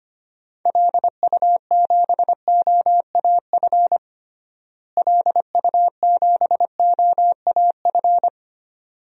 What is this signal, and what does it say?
Morse code, heard as a keyed tone, about 25 wpm, 700 Hz. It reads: LU7OAF LU7OAF
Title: DXCC-100 Call Signs - 6 Characters - Repeated 2x - 25wpm